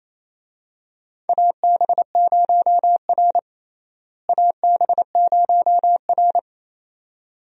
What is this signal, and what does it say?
Morse code, heard as a keyed tone, about 28 wpm, 700 Hz. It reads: AB0R AB0R
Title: USA Call Signs - 4 Characters - Repeated 2x - 28wpm